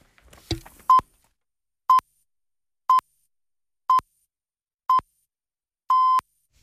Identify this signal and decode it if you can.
Radio time signal (time pips). It is heard from 0.9 to 6.2 s.